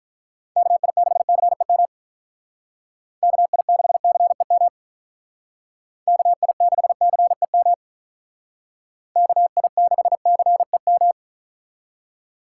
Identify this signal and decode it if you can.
Morse code, heard as a keyed tone, about 46 wpm, 700 Hz. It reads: KI6CEM KI6CEM KI6CEM KI6CEM